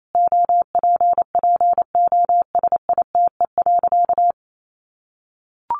Morse code, heard as a keyed tone, about 28 wpm, 700 Hz. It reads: OPPOSITE.
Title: Sentences from Top 1300 Words 28wpm